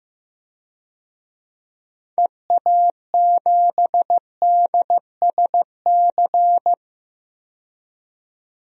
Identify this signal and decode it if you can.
Morse code, heard as a keyed tone, about 15 wpm, 690 Hz. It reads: EA7DSC
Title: DXCC-100 Call Signs - 6 Characters - 15wpm